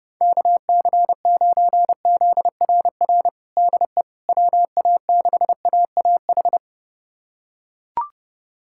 Morse code, heard as a keyed tone, about 30 wpm, 700 Hz. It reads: KC9ZRR DE WA6AAH
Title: USA Callsign DE Callsign 30wpm